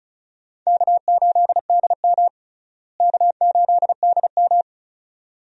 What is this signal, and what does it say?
Morse code, heard as a keyed tone, about 34 wpm, 700 Hz. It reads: K8DM K8DM
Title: USA Call Signs - 4 Characters - Repeated 2x - 35wpm